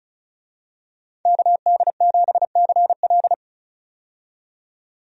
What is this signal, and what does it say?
Morse code, heard as a keyed tone, about 36 wpm, 700 Hz. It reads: KD7CL
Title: USA Call Signs - 5 Characters - 35wpm